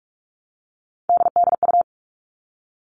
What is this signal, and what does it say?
Morse code, heard as a keyed tone, about 44 wpm, 700 Hz. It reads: DDU
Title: All Cut Numbers - 3-Digit 45wpm